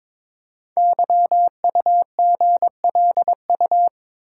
Morse code, heard as a keyed tone, about 22 wpm, 700 Hz. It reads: YUGLU
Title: Code Groups - 5 Letter:Numbers - 22wpm